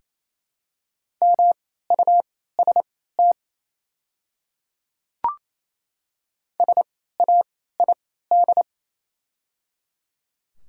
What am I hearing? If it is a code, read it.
Morse code, heard as a keyed tone, about 27 wpm, spaced as for 16 wpm, 700 Hz. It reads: MUST SAID
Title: 3x Character Spacing - Top 300 Words - 28wpm